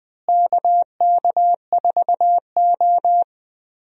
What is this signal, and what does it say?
Morse code, heard as a keyed tone, about 20 wpm, 700 Hz. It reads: KK4O
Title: USA Call Signs - 4 Characters - Repeated 2x - 20wpm